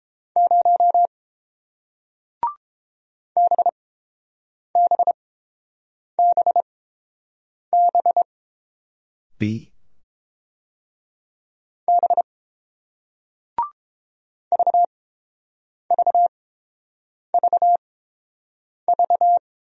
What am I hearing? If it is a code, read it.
Morse code, heard as a keyed tone, about 31 wpm, spaced as for 7 wpm, 700 Hz. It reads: 0 BBBB B VVVV